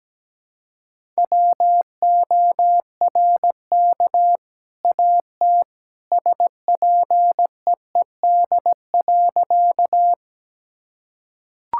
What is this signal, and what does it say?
Morse code, heard as a keyed tone, about 17 wpm, 700 Hz. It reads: WORK AT SPEED.
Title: Sentences from Top 1000 Words - Review Part 5 - 17wpm